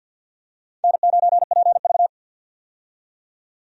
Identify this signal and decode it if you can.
Morse code, heard as a keyed tone, about 45 wpm, 700 Hz. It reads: N9PV
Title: USA Call Signs - 4 Characters - 50wpm